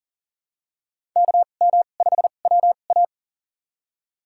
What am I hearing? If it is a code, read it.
Morse code, heard as a keyed tone, about 40 wpm, spaced as for 29 wpm, 700 Hz. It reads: KM5WA
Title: USA Call Signs - 5 Characters - 2x Character Spacing - 40wpm